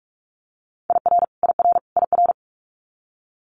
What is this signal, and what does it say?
Morse code, heard as a keyed tone, about 46 wpm, 700 Hz. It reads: IR IR IR
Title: Multi-letter Phonemes Repeated 3x 45wpm